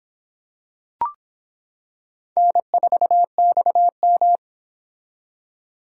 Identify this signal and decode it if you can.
Morse code, heard as a keyed tone, about 24 wpm, 700 Hz. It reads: N4XM